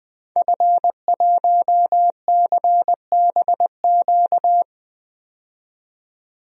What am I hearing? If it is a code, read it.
Morse code, heard as a keyed tone, about 20 wpm, 700 Hz. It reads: F1CBQ
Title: DXCC-100 Call Signs - 5 Characters - 20wpm